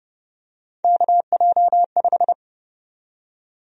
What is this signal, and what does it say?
Morse code, heard as a keyed tone, about 30 wpm, 700 Hz. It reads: KJ5
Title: USA Call Sign Prefixes - 30wpm